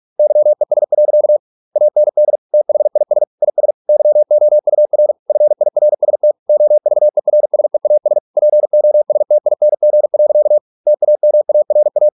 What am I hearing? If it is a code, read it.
Morse code, heard as a keyed tone, about 46 wpm, 600 Hz. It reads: YES. AND THIS IS YOUR FIRST OVERSEAS POSTING. TAMARA